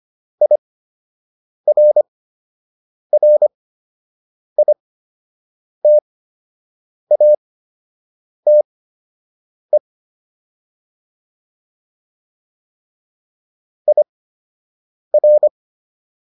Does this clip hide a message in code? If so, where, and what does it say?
Yes: Morse code from 0.4 to 15.5 s, heard as a keyed tone, about 25 wpm, spaced as for 7 wpm, 600 Hz. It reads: IRRITATE IR